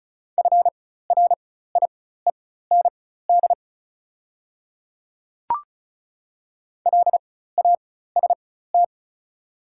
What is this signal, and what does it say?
Morse code, heard as a keyed tone, about 36 wpm, spaced as for 16 wpm, 700 Hz. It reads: FRIEND LAST
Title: 4x Character Spacing - Top 300 Words - 35wpm